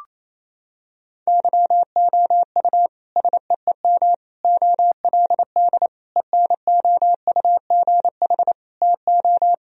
Morse code, heard as a keyed tone, about 28 wpm, 700 Hz. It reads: YOU SEEM OLD ENOUGH TO
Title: Sentences from Top 1000 Words - Review Part 1 - 28wpm